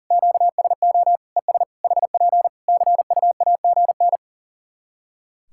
Morse code, heard as a keyed tone, about 37 wpm, 700 Hz. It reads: QSO ES HP CUAGN
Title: Practice QSO - Exchange 4 - Ending - 40wpm